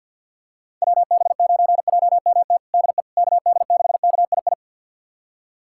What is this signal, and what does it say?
Morse code, heard as a keyed tone, about 50 wpm, 700 Hz. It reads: WB9JMT DE KD6KII
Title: USA Callsign DE Callsign - 6 Characters - 50wpm